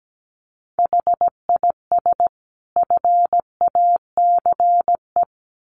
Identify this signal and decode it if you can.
Morse code, heard as a keyed tone, about 17 wpm, 700 Hz. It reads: HIS FACE
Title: A Sound Of Thunder 17wpm